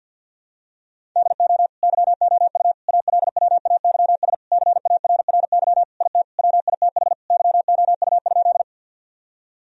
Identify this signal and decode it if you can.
Morse code, heard as a keyed tone, about 49 wpm, 700 Hz. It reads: DO YOU ALWAYS CARRY IT WITH YOU?
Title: Sentences from Top 1000 Words - Review Part 1 - 50wpm